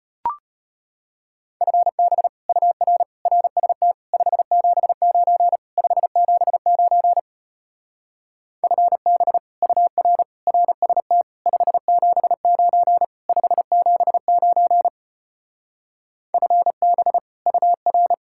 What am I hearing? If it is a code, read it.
Morse code, heard as a keyed tone, about 33 wpm, 700 Hz. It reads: FB UR RST 579 579 FB UR RST 579 579 FB UR